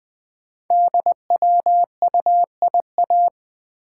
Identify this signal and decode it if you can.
Morse code, heard as a keyed tone, about 20 wpm, 700 Hz. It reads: DWUIA
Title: Code Groups - 5 Letter:Numbers - 20wpm